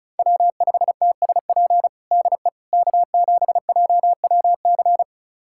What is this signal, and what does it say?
Morse code, heard as a keyed tone, about 36 wpm, 700 Hz. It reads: W5TSP DE K7JWC
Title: USA Callsign DE Callsign - 5 Characters - 35wpm